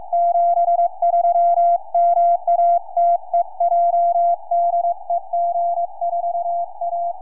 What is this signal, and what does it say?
Morse code, heard as a keyed tone, about 22 wpm, 690 Hz. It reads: 73 MATEJ DE G4A
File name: g4asr 4m